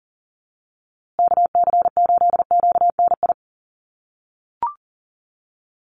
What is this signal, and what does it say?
Morse code, heard as a keyed tone, about 40 wpm, 700 Hz. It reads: KC8QNI
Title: USA Call Signs Repeated 2x 40wpm